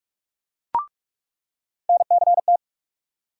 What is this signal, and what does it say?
Morse code, heard as a keyed tone, about 46 wpm, 700 Hz. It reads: NCT
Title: USA Call Sign Postfixes - 3 Letters 45wpm